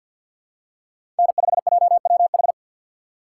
Morse code, heard as a keyed tone, about 52 wpm, 700 Hz. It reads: N5JWH
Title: USA Call Signs - 5 Characters - 4x Character Spacing - 50wpm